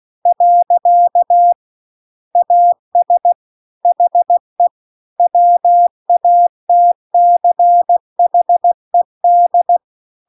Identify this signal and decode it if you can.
Morse code, heard as a keyed tone, about 16 wpm, 700 Hz. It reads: . AS HE WATCHED
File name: CW-The-Big-Twoheaded-River-1-16WPM